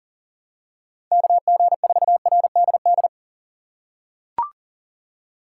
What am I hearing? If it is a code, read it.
Morse code, heard as a keyed tone, about 38 wpm, 700 Hz. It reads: KG4RDD
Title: USA Call Signs - 6 Characters - 3x Character Spacing - 40wpm